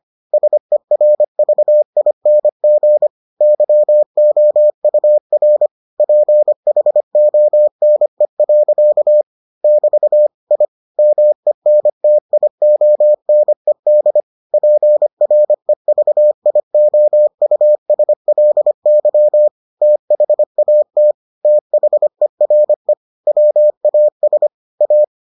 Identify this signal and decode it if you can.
Morse code, heard as a keyed tone, about 25 wpm, 600 Hz. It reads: SERVING YOUR PHONE. <BT> I MENTIONED PREVIOUSLY THAT THERE WAS A